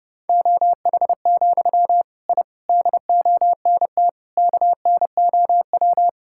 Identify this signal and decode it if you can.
Morse code, heard as a keyed tone, about 32 wpm, 700 Hz. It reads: OH, I DONT KNOW
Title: Sentences from Top 1000 Words - Review Part 6 - 30wpm